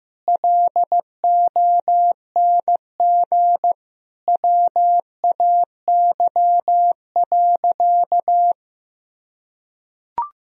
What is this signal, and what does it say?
Morse code, heard as a keyed tone, about 15 wpm, 700 Hz. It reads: LONG WAY.
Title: Sentences from Top 1000 Words - Review Part 1 - 15wpm